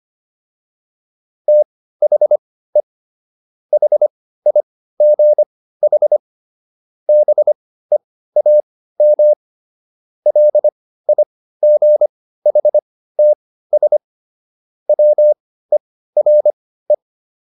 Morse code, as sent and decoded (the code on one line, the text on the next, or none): - .... / .... .. --. .... / -... .- -- / .-.. .. --. .... - ... / .-- . .-. .
THE HIGH BEAM LIGHTS WERE